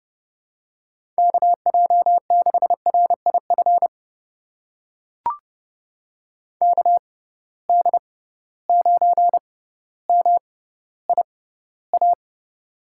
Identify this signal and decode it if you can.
Morse code, heard as a keyed tone, about 31 wpm, spaced as for 15 wpm, 700 Hz. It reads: KJ6RIF KD9MIA
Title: USA Call Signs - 6 Characters - 6x Character Spacing - 30wpm